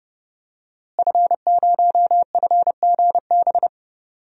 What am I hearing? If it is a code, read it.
Morse code, heard as a keyed tone, about 30 wpm, 700 Hz. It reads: F0FGB
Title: DXCC-100 Call Signs - 5 Characters - Repeated 3x - 30wpm